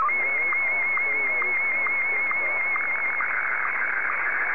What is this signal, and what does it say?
SSTV (slow-scan television), Martin 1, with speech.
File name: sstv / M1-VK3